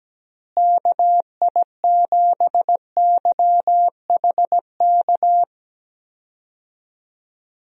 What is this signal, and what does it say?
Morse code, heard as a keyed tone, about 17 wpm, 700 Hz. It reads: KI7YHK
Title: USA Call Signs - 6 Characters - Repeated 3x - 17wpm